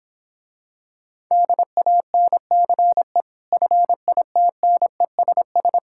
Morse code, heard as a keyed tone, about 24 wpm, 700 Hz. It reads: DANCE FITNESS
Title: Difficult Words - Speed-Racing with Hints - 20wpm